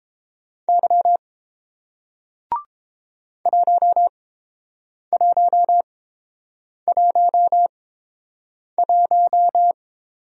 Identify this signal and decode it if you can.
Morse code, heard as a keyed tone, about 30 wpm, spaced as for 7 wpm, 700 Hz. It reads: Y 1111